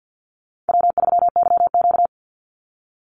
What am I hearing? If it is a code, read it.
Morse code, heard as a keyed tone, about 48 wpm, 700 Hz. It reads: W3YQ
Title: Easy Call Signs Repeated 3x - 50wpm